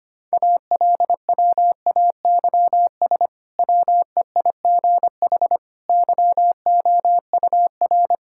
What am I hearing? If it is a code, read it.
Morse code, heard as a keyed tone, about 25 wpm, 700 Hz. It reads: ALWAYS WEIGH YOUR
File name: practice / 25wpm / Difficult Sentences with Hints - 25wpm